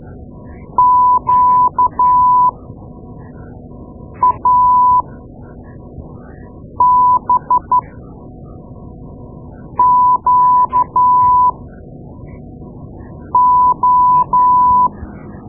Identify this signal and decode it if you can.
Morse code, heard as a keyed tone, about 12 wpm, spaced as for 4 wpm, 1000 Hz. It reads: QABQO
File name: Morse Lessons / Morse Lessons / Tape-2A